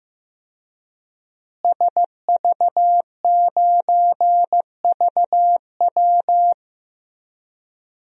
Morse code, heard as a keyed tone, about 15 wpm, 700 Hz. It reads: SV9VW
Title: DXCC-100 Call Signs - 5 Characters - Repeated 3x - 15wpm